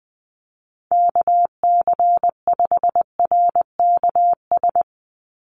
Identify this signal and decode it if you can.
Morse code, heard as a keyed tone, about 20 wpm, 700 Hz. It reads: KC5RKS
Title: USA Call Signs 20wpm